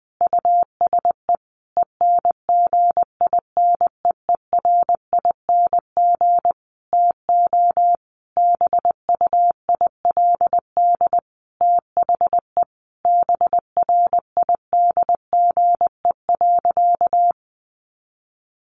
Morse code, heard as a keyed tone, about 20 wpm, 700 Hz. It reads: USE ENGINEERING TO BUILD THE BRIDGE.
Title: Sentences from Top 1500 Words 20wpm